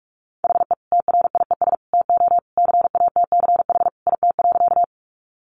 Morse code, heard as a keyed tone, about 44 wpm, 700 Hz. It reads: HE TRIES TO CATCH IT.
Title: Additional Easy Phrases 45wpm